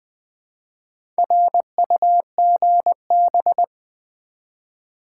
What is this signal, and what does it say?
Morse code, heard as a keyed tone, about 20 wpm, 700 Hz. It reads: RUGB
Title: Build A Word - 1 Letter At A Time - 3 to 5-Letter Words - 20wpm